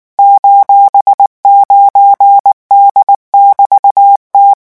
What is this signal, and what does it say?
Morse code, heard as a keyed tone, about 19 wpm, 800 Hz. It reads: :9D<BT>T
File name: morse mp3 / revision14-vitesse 18 mots